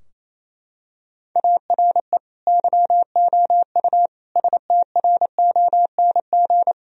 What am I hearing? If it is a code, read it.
Morse code, heard as a keyed tone, about 28 wpm, 700 Hz. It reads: ARE YOU STRONG
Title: Sentences from Top 1000 Words - Review Part 3 - 28wpm